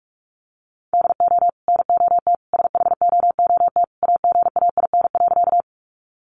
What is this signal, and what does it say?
Morse code, heard as a keyed tone, about 44 wpm, 700 Hz. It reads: DO NOT SHOOT AGAIN.
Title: Sentences from Top 1200 Words 45wpm